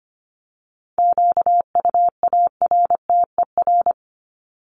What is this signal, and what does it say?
Morse code, heard as a keyed tone, about 25 wpm, 700 Hz. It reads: QUARTER
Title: Top 1001-1250 Words 25wpm